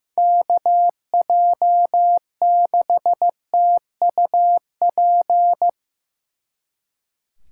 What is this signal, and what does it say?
Morse code, heard as a keyed tone, about 16 wpm, 700 Hz. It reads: KJ6TUP
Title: Speed Racing - USA 2x3 Call Signs 15wpm